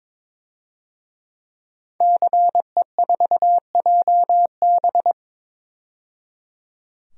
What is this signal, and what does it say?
Morse code, heard as a keyed tone, about 22 wpm, 700 Hz. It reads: CE4JB
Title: DXCC-100 Call Signs - 5 Characters - 22wpm